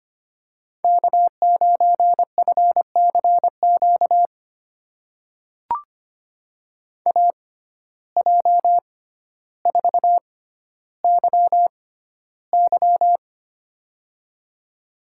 Morse code, heard as a keyed tone, about 25 wpm, spaced as for 13 wpm, 700 Hz. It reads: K9FCQ AJ4YY
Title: USA Call Signs - 5 Characters - 6x Character Spacing - 25wpm